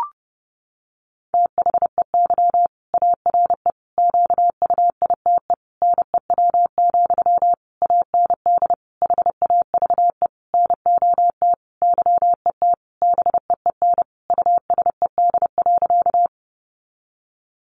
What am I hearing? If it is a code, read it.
Morse code, heard as a keyed tone, about 30 wpm, 700 Hz. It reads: THEY ARE QUITE NEW, AND HAVE NOT YET BEEN USED.
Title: Sentences from Top 700 Words 30wpm